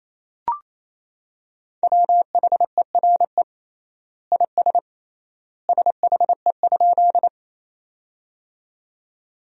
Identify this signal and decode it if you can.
Morse code, heard as a keyed tone, about 28 wpm, 700 Hz. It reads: WHERE IS SHE?
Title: 3x Word-Spacing - Sentences from Top 200 Words - 28wpm